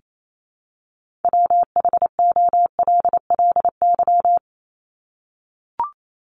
Morse code, heard as a keyed tone, about 28 wpm, 700 Hz. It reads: WHOLLY
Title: LY Suffix - Top 100 Words with Base Words 28wpm